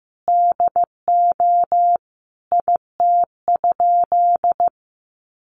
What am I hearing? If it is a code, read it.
Morse code, heard as a keyed tone, about 15 wpm, 700 Hz. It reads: DO IT?
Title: Sentences from Top 1300 Words 15wpm